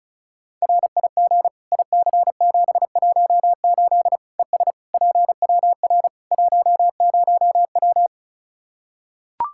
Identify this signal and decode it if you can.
Morse code, heard as a keyed tone, about 36 wpm, 700 Hz. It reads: RIG IC718 ES PWR 10W
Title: Practice QSO - Exchange 2 - Info - 35wpm